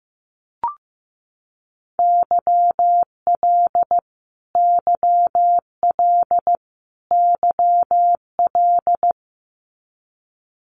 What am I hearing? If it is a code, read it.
Morse code, heard as a keyed tone, about 15 wpm, 700 Hz. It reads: YL YL YL
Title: Abbreviations Repeated 3x - 15wpm